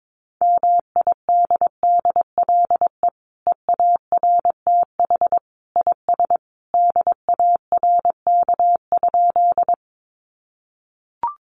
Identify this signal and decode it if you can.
Morse code, heard as a keyed tone, about 22 wpm, 700 Hz. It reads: MIDDLE EARTH IS DARK?
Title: Sentences from Top 900 Words 22wpm